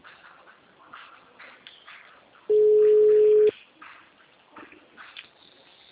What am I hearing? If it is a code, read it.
Ringback tone (Europe, 425 Hz).